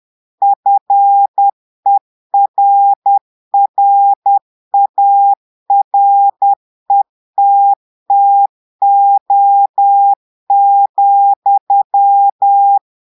Morse code, heard as a keyed tone, about 10 wpm, 800 Hz. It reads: FERRARETTO,
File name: Portugais - texte en clair 1 - vitesse - 10 mots minute